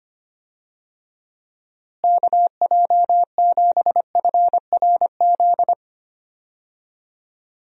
Morse code, heard as a keyed tone, about 25 wpm, 700 Hz. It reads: KJ7FRZ
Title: USA Call Signs - 6 Characters - Repeated 3x - 25wpm